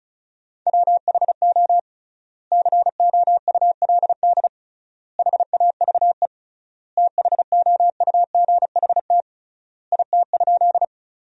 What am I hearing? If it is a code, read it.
Morse code, heard as a keyed tone, about 36 wpm, 700 Hz. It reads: WHO COULD HAVE THOUGHT IT?
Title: 3x Word-Spacing - Sentences from Top 300 Words - 35wpm